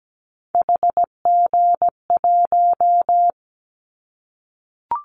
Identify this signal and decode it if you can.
Morse code, heard as a keyed tone, about 17 wpm, 700 Hz. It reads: HG1
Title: Callsign Prefixes for Top 25 DXCC - Repeated 3x - 17wpm